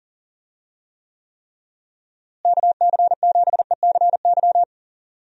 Morse code, heard as a keyed tone, about 40 wpm, 700 Hz. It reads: KC7ECY
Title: USA Call Signs - 6 Characters - Repeated 3x - 40wpm